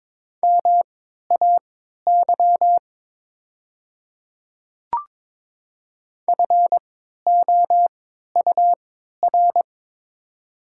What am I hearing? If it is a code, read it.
Morse code, heard as a keyed tone, about 22 wpm, spaced as for 13 wpm, 700 Hz. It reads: MAY FOUR
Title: 3x Character Spacing - Top 300 Words - 22wpm